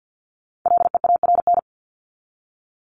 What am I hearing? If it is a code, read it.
Morse code, heard as a keyed tone, about 50 wpm, 700 Hz. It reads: LEARN
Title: ING Suffix - Top 100 Words with Base Words 50wpm